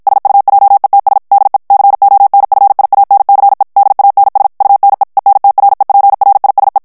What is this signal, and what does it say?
Morse code, heard as a keyed tone, about 56 wpm, 800 Hz. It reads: SUJETS DE CONFIANCE DANS UNE ENTREPRISE
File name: FRANCAIS TEXTE CLAIR 2 VITESSE 50 MOTS MINUTES